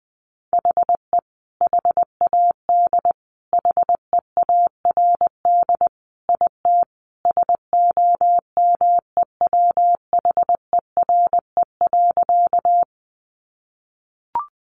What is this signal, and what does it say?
Morse code, heard as a keyed tone, about 20 wpm, 700 Hz. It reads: HE HAD HEARD IT SOMEWHERE.
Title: Sentences from Top 1400 Words 20wpm